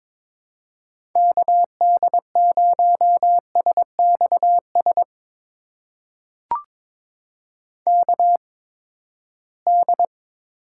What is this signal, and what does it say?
Morse code, heard as a keyed tone, about 22 wpm, spaced as for 15 wpm, 700 Hz. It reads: KD0SXS KD
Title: USA Call Signs - 6 Characters - 8x Character Spacing - 22wpm